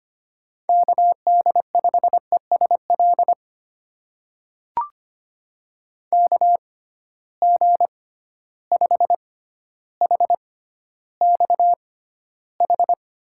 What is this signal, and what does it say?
Morse code, heard as a keyed tone, about 25 wpm, spaced as for 13 wpm, 700 Hz. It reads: KD5ESL KG5HXH
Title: USA Call Signs - 6x Character Spacing - 25wpm